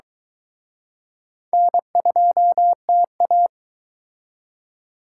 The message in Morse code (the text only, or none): N2TA